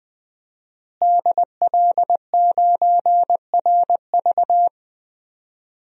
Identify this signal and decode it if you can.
Morse code, heard as a keyed tone, about 20 wpm, 700 Hz. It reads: DL9RV
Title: DXCC-100 Call Signs - 5 Characters - Repeated 2x - 20wpm